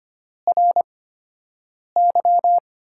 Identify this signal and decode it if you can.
Morse code, heard as a keyed tone, about 25 wpm, spaced as for 7 wpm, 700 Hz. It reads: RY
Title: USA Call Signs - 8x Character Spacing - 25wpm